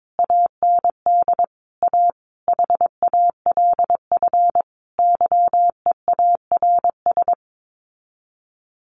Morse code, heard as a keyed tone, about 22 wpm, 700 Hz. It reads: AND A HALF YEARS
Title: Sets of 5 Words 22wpm